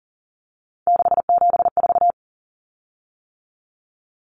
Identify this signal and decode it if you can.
Morse code, heard as a keyed tone, about 40 wpm, 700 Hz. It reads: 674
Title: Common Cut Numbers - 3-Digit 40wpm